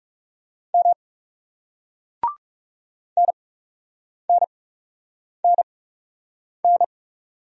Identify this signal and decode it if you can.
Morse code, heard as a keyed tone, about 39 wpm, spaced as for 8 wpm, 700 Hz. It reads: M NNNN